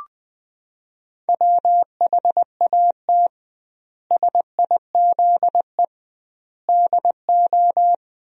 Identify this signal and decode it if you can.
Morse code, heard as a keyed tone, about 20 wpm, 700 Hz. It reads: WHAT SIZE DO
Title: 2x Word-Spacing - Sentences from Top 400 Words - 20wpm